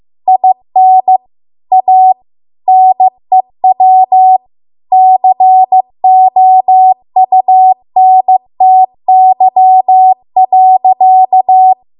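Morse code, heard as a keyed tone, about 15 wpm, 750 Hz. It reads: IN A NEW COUNTY.
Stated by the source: Morse code practice at 15 wpm from the national amateur radio association's code archive.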